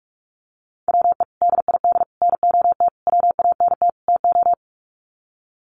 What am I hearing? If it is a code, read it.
Morse code, heard as a keyed tone, about 44 wpm, 700 Hz. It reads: WE DID NOT WANT TO
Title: Sets of 5 Words Encore 45wpm